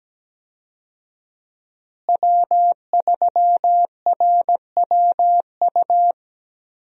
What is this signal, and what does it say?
Morse code, heard as a keyed tone, about 17 wpm, 700 Hz. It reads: W3RWU
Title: USA Call Signs - 5 Characters - Repeated 3x - 17wpm